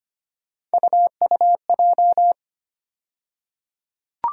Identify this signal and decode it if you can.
Morse code, heard as a keyed tone, about 25 wpm, 700 Hz. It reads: UUJ
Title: USA Call Sign Postfixes - 3 Letters 25wpm